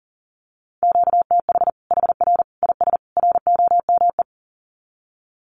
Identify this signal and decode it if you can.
Morse code, heard as a keyed tone, about 40 wpm, 700 Hz. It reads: QTH HR IS ROME
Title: Common QSO Phrases 40wpm